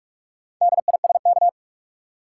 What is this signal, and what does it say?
Morse code, heard as a keyed tone, about 48 wpm, 700 Hz. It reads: DISK
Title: Four Letter Words - 45wpm